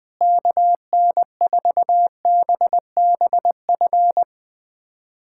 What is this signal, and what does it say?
Morse code, heard as a keyed tone, about 21 wpm, 700 Hz. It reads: KN4BBF
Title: USA Callsign DE Callsign - 6 Characters - 20wpm